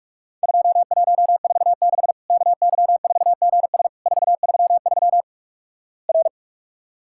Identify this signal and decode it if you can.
Morse code, heard as a keyed tone, about 44 wpm, 700 Hz. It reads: 2146 KY4GS 433 R
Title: Worked All Europe DX Contest - QTC Exchange - 45wpm